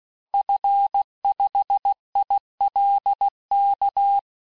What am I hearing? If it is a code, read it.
Morse code, heard as a keyed tone, about 16 wpm, 800 Hz. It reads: F5ILK
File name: cours-du-lundi files / cours-1592--teste-FIN-de-QSO-vitesse-15mpm